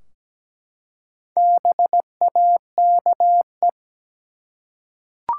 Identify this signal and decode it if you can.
Morse code, heard as a keyed tone, about 17 wpm, 700 Hz. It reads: BAKE
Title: Four Letter Words - 17wpm